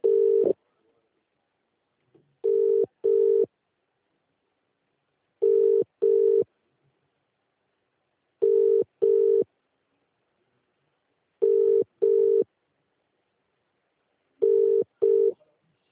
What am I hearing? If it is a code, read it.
Ringback tone (UK), 400 + 450 Hz.